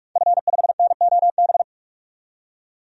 Morse code, heard as a keyed tone, about 46 wpm, 700 Hz. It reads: F5NOB
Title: DXCC-100 Call Signs - 5 Characters - 45wpm